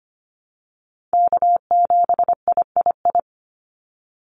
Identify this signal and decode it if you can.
Morse code, heard as a keyed tone, about 25 wpm, 700 Hz. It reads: K7III
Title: NA SOTA Chaser Callsigns 25wpm